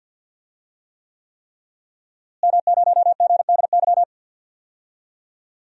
Morse code, heard as a keyed tone, about 53 wpm, 700 Hz. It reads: M0GDY
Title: DXCC-100 Call Signs - 5 Characters - Repeated 3x - 50wpm